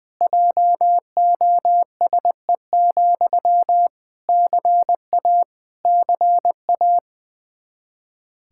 Morse code, heard as a keyed tone, about 21 wpm, 700 Hz. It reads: JOSE, CA CA
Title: Practice QSO - Exchange 1 - Beginning - 20wpm